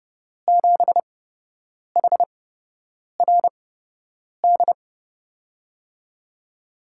Morse code, heard as a keyed tone, about 31 wpm, spaced as for 8 wpm, 700 Hz. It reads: 7HRD
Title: USA Call Signs - 6 Characters - 8x Character Spacing - 30wpm